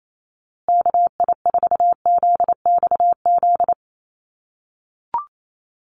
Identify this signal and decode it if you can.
Morse code, heard as a keyed tone, about 28 wpm, 700 Hz. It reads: KI4ZXZ
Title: USA Call Signs Repeated 2x 28wpm